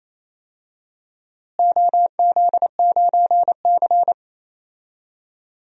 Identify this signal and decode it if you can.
Morse code, heard as a keyed tone, about 27 wpm, 700 Hz. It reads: OZ9C
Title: DXCC-100 Call Signs - 4 Characters - Repeated 2x - 28wpm